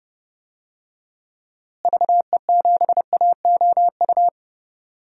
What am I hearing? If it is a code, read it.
Morse code, heard as a keyed tone, about 32 wpm, 700 Hz. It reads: VE7AOU
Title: DXCC-100 Call Signs - 6 Characters - Repeated 2x - 30wpm